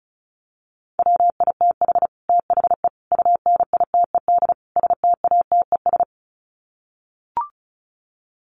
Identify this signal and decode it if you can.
Morse code, heard as a keyed tone, about 34 wpm, 700 Hz. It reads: WITH THE UNITED STATES
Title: Sets of 4 Words 35wpm